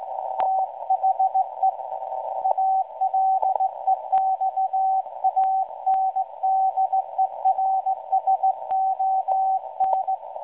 Morse code, heard as a keyed tone, about 16 wpm, 760 Hz, with radio noise. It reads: THE ALEXANDERSON
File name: WebSDR records / SAQ at PI4THT msg 0800-0804 17.2kHz